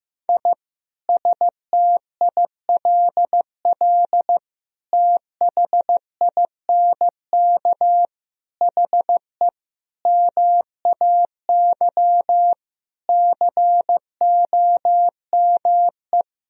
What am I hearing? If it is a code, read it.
Morse code, heard as a keyed tone, about 15 wpm, 700 Hz. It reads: I STILL THINK HE MAY COME
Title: Sentences from Top 1000 Words - Review Part 1 - 15wpm